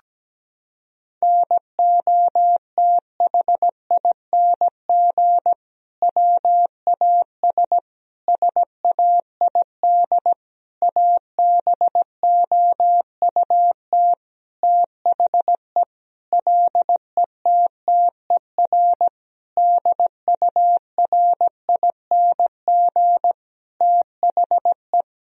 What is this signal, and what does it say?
Morse code, heard as a keyed tone, about 17 wpm, 700 Hz. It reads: NOTHING WAS SAID ABOUT THE LETTER DURING THE
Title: Sentences from Top 1000 Words - Review Part 2 - 17wpm